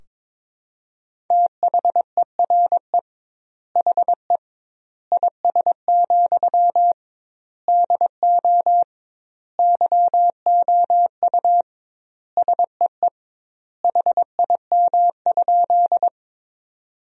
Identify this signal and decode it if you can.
Morse code, heard as a keyed tone, about 21 wpm, 700 Hz. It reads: THERE HE IS, DO YOU SEE HIM?
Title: 2x Word-Spacing - Sentences from Top 100 Words - 22wpm